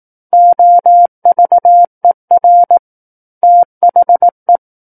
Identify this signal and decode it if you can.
Morse code, heard as a keyed tone, about 18 wpm, 700 Hz. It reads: OVER THE
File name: 18 WPM